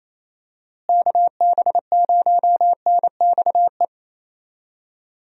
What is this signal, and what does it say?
Morse code, heard as a keyed tone, about 28 wpm, 700 Hz. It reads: KB0NXE
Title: USA Call Signs - 6 Characters - Repeated 3x - 28wpm